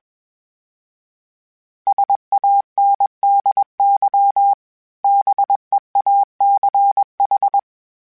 Morse code, heard as a keyed tone, about 20 wpm, 800 Hz. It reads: SANDY BEACH
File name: fromSMs / FromSMs mp3s / Cities@20